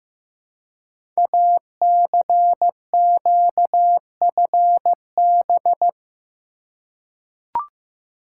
Morse code, heard as a keyed tone, about 15 wpm, 700 Hz. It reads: ACQFB
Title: Code Groups - 5 Letters - 15wpm